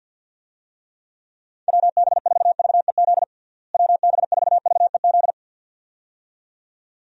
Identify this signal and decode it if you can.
Morse code, heard as a keyed tone, about 51 wpm, 700 Hz. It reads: WB4VEZ WB4VEZ
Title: USA Call Signs - 6 Characters - Repeated 2x - 50wpm